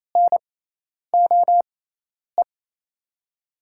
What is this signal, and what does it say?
Morse code, heard as a keyed tone, about 28 wpm, spaced as for 10 wpm, 700 Hz. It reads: NOE
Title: USA Call Signs - 5 Characters - 6x Character Spacing - 28wpm